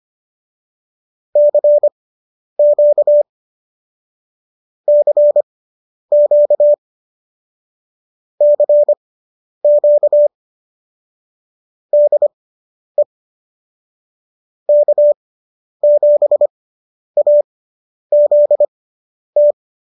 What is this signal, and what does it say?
Morse code, heard as a keyed tone, about 25 wpm, spaced as for 10 wpm, 590 Hz. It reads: CQ CQ CQ DE K7AZT